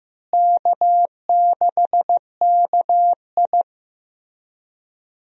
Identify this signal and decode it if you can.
Morse code, heard as a keyed tone, about 16 wpm, 700 Hz. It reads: K6KI
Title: USA Call Signs - 4 Characters - Repeated 3x - 15wpm